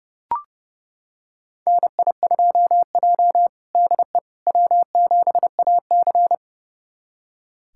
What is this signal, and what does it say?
Morse code, heard as a keyed tone, about 30 wpm, 700 Hz. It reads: NI2J DE W7AC
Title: USA Callsign DE Callsign - 4 Characters - 30wpm